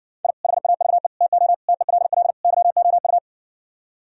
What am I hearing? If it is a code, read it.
Morse code, heard as a keyed tone, about 60 wpm, 700 Hz. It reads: I HAVE TO TELL YOU